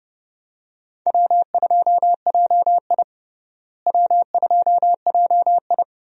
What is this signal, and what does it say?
Morse code, heard as a keyed tone, about 30 wpm, 700 Hz. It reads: W2JI W2JI
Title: USA Call Signs - 4 Characters - Repeated 2x - 30wpm